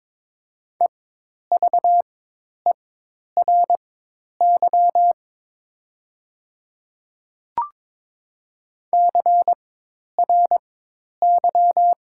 Morse code, heard as a keyed tone, about 22 wpm, spaced as for 10 wpm, 700 Hz. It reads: EVERY CRY
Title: 4x Character Spacing - Top 500 Words - 22wpm